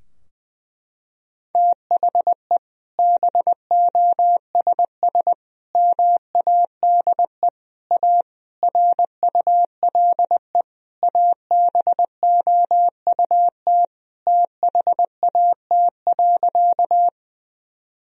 Morse code, heard as a keyed tone, about 21 wpm, 700 Hz. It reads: THE BOSS MADE A RULE ABOUT THAT.
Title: Sentences from Top 1000 Words - Review Part 4 - 20wpm